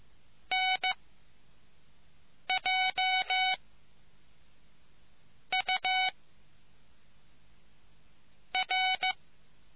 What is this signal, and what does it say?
Morse code, heard as a keyed tone, about 15 wpm, 2250 Hz. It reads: N J U R